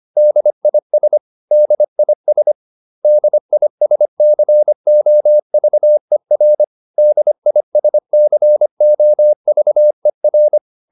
Morse code, heard as a keyed tone, about 25 wpm, 600 Hz. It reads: DIS DIS DISCOVER DISCOVER
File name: DIS8 25